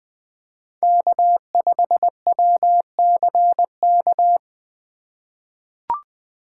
Morse code, heard as a keyed tone, about 20 wpm, 700 Hz. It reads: K5WCK
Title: USA Call Signs - 5 Characters - 3x Character Spacing - 20wpm